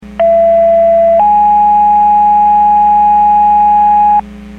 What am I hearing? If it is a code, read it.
Two-tone page, heard as a one-second tone, then a three-second tone, about 670 then 830 Hz, with noise. It is heard from 0.2 to 4.2 s.
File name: Portage 154.385 Paging